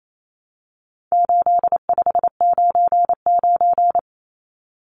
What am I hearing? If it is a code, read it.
Morse code, heard as a keyed tone, about 28 wpm, 700 Hz. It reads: 8599
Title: Numbers - 4-Digit 28wpm